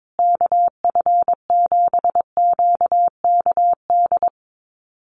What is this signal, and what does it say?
Morse code, heard as a keyed tone, about 22 wpm, 700 Hz. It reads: KF7QKD
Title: USA Call Signs Repeated 2x 22wpm